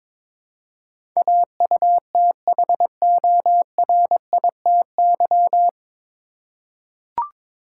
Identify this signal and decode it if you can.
Morse code, heard as a keyed tone, about 22 wpm, 700 Hz. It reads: AUTHORITY
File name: 9 Letter Words - First 3 Letters Given - 22wpm